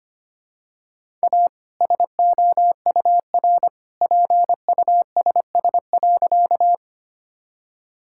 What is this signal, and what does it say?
Morse code, heard as a keyed tone, about 25 wpm, 700 Hz. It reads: A SOUR PUSS.
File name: practice / 25wpm / All-Age Appropriate Jokes - 25wpm